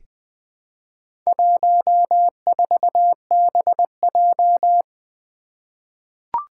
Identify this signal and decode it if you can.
Morse code, heard as a keyed tone, about 20 wpm, 700 Hz. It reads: 14BJ
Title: Four Challenging Letter-Number Combinations 20wpm